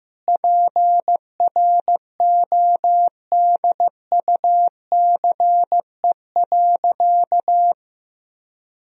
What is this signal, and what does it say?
Morse code, heard as a keyed tone, about 16 wpm, 700 Hz. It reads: PRODUCE.
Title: Sentences from Top 1000 Words - Review Part 3 - 15wpm